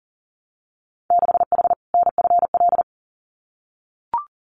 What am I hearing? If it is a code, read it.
Morse code, heard as a keyed tone, about 38 wpm, 700 Hz. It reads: 6H NFL
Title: Winter Field Day Exchanges Encore 40wpm